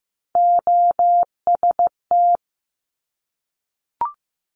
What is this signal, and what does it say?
Morse code, heard as a keyed tone, about 15 wpm, 700 Hz. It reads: OST
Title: Top Three Letter Combinations in Words 15wpm